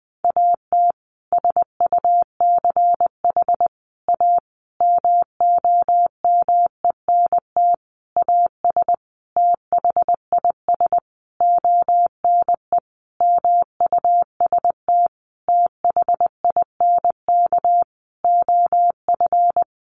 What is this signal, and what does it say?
Morse code, heard as a keyed tone, about 20 wpm, 700 Hz. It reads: AT SUCH A MOMENT AS THIS ONE MUST THINK OF
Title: Sentences from Top 1000 Words 20wpm